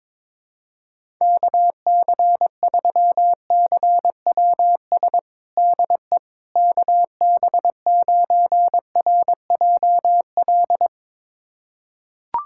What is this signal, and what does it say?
Morse code, heard as a keyed tone, about 22 wpm, 700 Hz. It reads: KC3CWS DE KB9RJL
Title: USA Callsign DE Callsign - 6 Characters - 22wpm